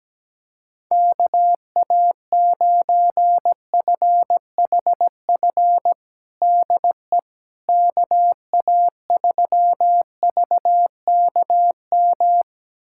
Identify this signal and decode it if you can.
Morse code, heard as a keyed tone, about 17 wpm, 700 Hz. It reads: KA9FHF DE KA3VKM
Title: USA Callsign DE Callsign - 6 Characters - 17wpm